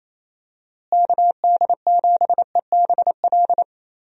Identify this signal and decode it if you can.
Morse code, heard as a keyed tone, about 28 wpm, 700 Hz. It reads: KD7EBL
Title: USA Call Signs - 6 Characters - Repeated 3x - 28wpm